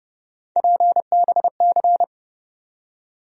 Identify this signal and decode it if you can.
Morse code, heard as a keyed tone, about 30 wpm, 700 Hz. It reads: PBC
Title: USA Call Sign Postfixes - 3 Letters 30wpm